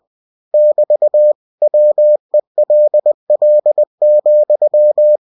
Morse code, heard as a keyed tone, about 20 wpm, 600 Hz. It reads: <BT> WELL,
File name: cw-books / winnie-the-pooh / winnie-20wpm-page-0036